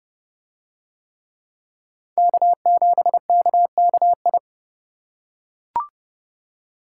Morse code, heard as a keyed tone, about 27 wpm, 700 Hz. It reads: K7KKI